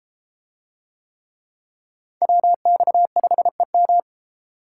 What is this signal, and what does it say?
Morse code, heard as a keyed tone, about 32 wpm, 700 Hz. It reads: WX5EM